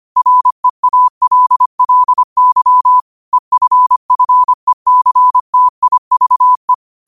Morse code, heard as a keyed tone, about 25 wpm, 1000 Hz. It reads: REALLY EFFECTIVE
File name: qotdfiles / 25wpm